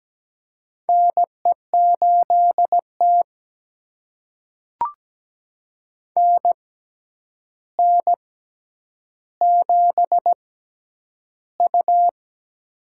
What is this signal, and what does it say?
Morse code, heard as a keyed tone, about 17 wpm, spaced as for 9 wpm, 700 Hz. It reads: NE8T NN7U
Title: USA Call Signs - 4 Characters - 6x Character Spacing - 17wpm